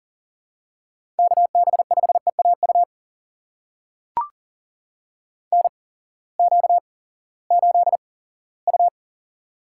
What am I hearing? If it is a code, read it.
Morse code, heard as a keyed tone, about 40 wpm, spaced as for 20 wpm, 700 Hz. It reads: KB5EAU NQ8U
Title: USA Call Signs - 8x Character Spacing - 40wpm